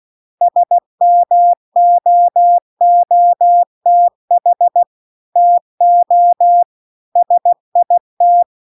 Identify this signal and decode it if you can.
Morse code, heard as a keyed tone, about 16 wpm, 700 Hz. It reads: SMOOTH TO SIT